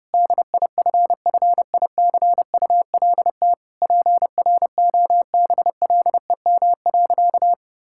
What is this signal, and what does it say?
Morse code, heard as a keyed tone, about 32 wpm, 700 Hz. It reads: DIFFICULT PROBLEM.
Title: Sentences from Top 1000 Words - Review Part 6 - 30wpm